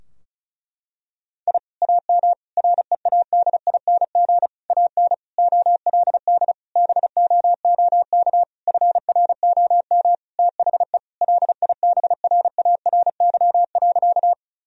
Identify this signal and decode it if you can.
Morse code, heard as a keyed tone, about 34 wpm, 700 Hz. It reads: I AM READING AN OLD BOOK FROM THE LIBRARY.
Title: Sentences from Top 1000 Words - Review Part 4 - 35wpm